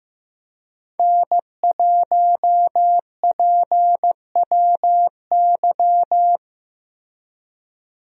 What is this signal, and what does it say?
Morse code, heard as a keyed tone, about 15 wpm, 700 Hz. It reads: N1PWY
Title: USA Call Signs - 5 Characters - Repeated 2x - 15wpm